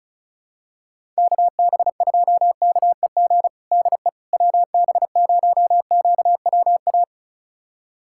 Morse code, heard as a keyed tone, about 36 wpm, 700 Hz. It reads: KB2KEG DE WB0QWA
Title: USA Callsign DE Callsign 35wpm